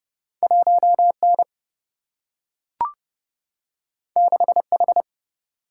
Morse code, heard as a keyed tone, about 30 wpm, 700 Hz. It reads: 1N 6H
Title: Single-Letter Single-Number 30wpm